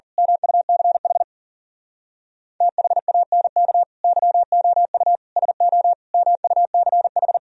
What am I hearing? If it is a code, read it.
Morse code, heard as a keyed tone, about 38 wpm, 700 Hz. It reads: MUCH THANK YOU SO MUCH
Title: Speed-Racing - Sets of 4 Words 40wpm